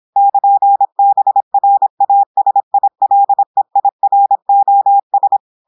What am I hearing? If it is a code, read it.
Morse code, heard as a keyed tone, about 26 wpm, 800 Hz. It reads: <KN>BRASILEIROS
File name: audios / portugais / portugais texte en clair 1 vitesse 26 mots minutes